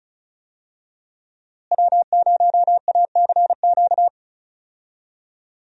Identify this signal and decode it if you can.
Morse code, heard as a keyed tone, about 36 wpm, 700 Hz. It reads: W0ACQ
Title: USA Call Signs - 5 Characters - Repeated 3x - 35wpm